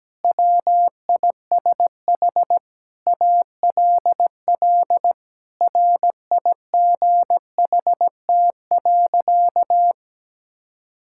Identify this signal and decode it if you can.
Morse code, heard as a keyed tone, about 17 wpm, 700 Hz. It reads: WISH ALL RIGHT.
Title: Sentences from Top 1600 Words 17wpm